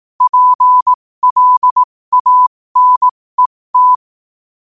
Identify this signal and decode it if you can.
Morse code, heard as a keyed tone, about 18 wpm, spaced as for 15 wpm, 1000 Hz. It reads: PLANET
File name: qotdfiles / 15wpm